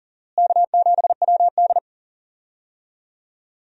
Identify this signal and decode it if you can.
Morse code, heard as a keyed tone, about 40 wpm, 700 Hz. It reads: K7WD
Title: USA Call Signs - 4 Characters - Repeated 3x - 40wpm